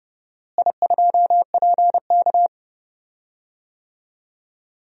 Morse code, heard as a keyed tone, about 30 wpm, 700 Hz. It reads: I2PK
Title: DXCC-100 Call Signs - 4 Characters - Repeated 3x - 30wpm